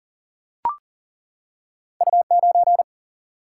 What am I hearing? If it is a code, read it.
Morse code, heard as a keyed tone, about 40 wpm, 700 Hz. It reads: U9
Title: Single-Letter Single-Number 40wpm